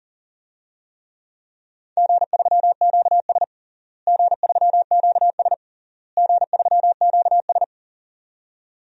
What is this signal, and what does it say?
Morse code, heard as a keyed tone, about 37 wpm, 700 Hz. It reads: G3QS G3QS G3QS
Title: DXCC-100 Call Signs - 4 Characters - Repeated 3x - 40wpm